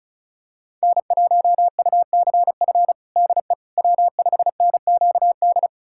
Morse code, heard as a keyed tone, about 36 wpm, 700 Hz. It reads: N1UCF DE W5NQD
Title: USA Callsign DE Callsign - 5 Characters - 35wpm